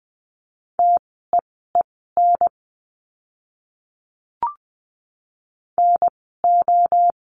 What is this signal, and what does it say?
Morse code, heard as a keyed tone, about 20 wpm, spaced as for 14 wpm, 700 Hz. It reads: TEEN NO
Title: 036 Lesson 7f - Letter s - Words 20wpm